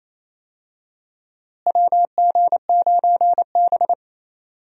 Morse code, heard as a keyed tone, about 28 wpm, 700 Hz. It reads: WG9B
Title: USA Call Signs - 4 Characters - Repeated 2x - 28wpm